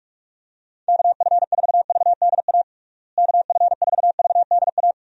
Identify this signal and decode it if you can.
Morse code, heard as a keyed tone, about 43 wpm, 700 Hz. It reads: KF4VDA KF4VDA
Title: USA Call Signs - 6 Characters - Repeated 2x - 45wpm